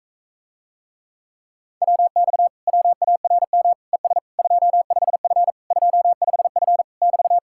Morse code, heard as a keyed tone, about 46 wpm, 700 Hz. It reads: WX WARM ES 25F 25F <BT>